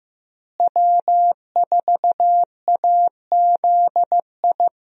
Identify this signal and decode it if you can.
Morse code, heard as a keyed tone, about 15 wpm, 700 Hz. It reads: W4AZI
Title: USA Call Signs - 5 Characters - Repeated 3x - 15wpm